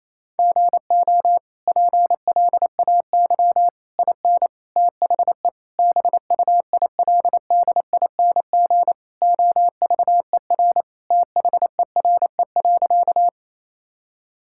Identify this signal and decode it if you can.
Morse code, heard as a keyed tone, about 28 wpm, 700 Hz. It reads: GO PLAY IN THE BUILDING OVER THERE.
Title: Sentences from Top 1000 Words - Review Part 5 - 28wpm